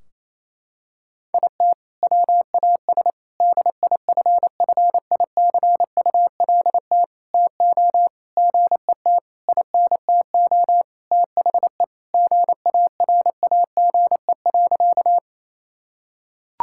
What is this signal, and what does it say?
Morse code, heard as a keyed tone, about 28 wpm, 700 Hz. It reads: IT WAS DIFFICULT TO GET INTO THE GARAGE.
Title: Sentences from Top 1600 Words 28wpm